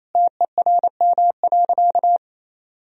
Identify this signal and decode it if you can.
Morse code, heard as a keyed tone, about 27 wpm, 700 Hz. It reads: TERM.
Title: Sentences from Top 1000 Words - Review Part 4 - 28wpm